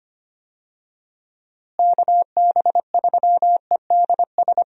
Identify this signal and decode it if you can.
Morse code, heard as a keyed tone, about 25 wpm, 700 Hz. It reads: KB3EDS
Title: USA Call Signs - 6 Characters - Repeated 3x - 25wpm